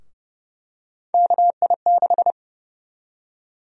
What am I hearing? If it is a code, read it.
Morse code, heard as a keyed tone, about 30 wpm, 700 Hz. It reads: KI6
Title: USA Call Sign Prefixes - 30wpm